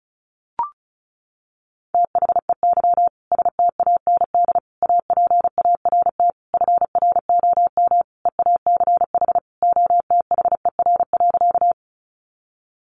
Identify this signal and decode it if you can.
Morse code, heard as a keyed tone, about 34 wpm, 700 Hz. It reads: THEY STAND APART FROM EACH OTHER.
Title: Easy Phrases 35wpm